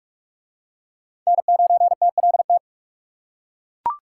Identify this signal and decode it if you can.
Morse code, heard as a keyed tone, about 42 wpm, 700 Hz. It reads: N9TLT